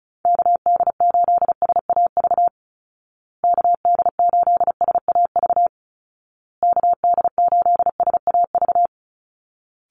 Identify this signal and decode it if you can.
Morse code, heard as a keyed tone, about 34 wpm, 700 Hz. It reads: KD8SAV KD8SAV KD8SAV
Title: Call Signs Repeated 2x - 35wpm